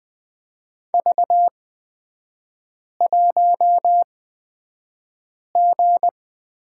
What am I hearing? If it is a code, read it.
Morse code, heard as a keyed tone, about 20 wpm, spaced as for 5 wpm, 700 Hz. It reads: V1G
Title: 250 Lesson 40e - BK - Three Characters 20wpm